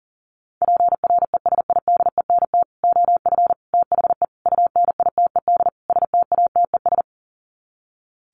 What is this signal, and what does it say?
Morse code, heard as a keyed tone, about 40 wpm, 700 Hz. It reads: PRESIDENT OF THE UNITED STATES
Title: Sets of 5 Words 40wpm